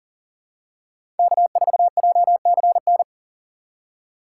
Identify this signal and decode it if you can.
Morse code, heard as a keyed tone, about 40 wpm, 700 Hz. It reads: K4JCN
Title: USA Call Signs - 5 Characters - 4x Character Spacing - 40wpm